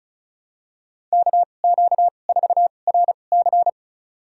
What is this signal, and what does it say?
Morse code, heard as a keyed tone, about 36 wpm, spaced as for 25 wpm, 700 Hz. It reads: KQ4RC
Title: USA Call Signs - 5 Characters - 2x Character Spacing - 35wpm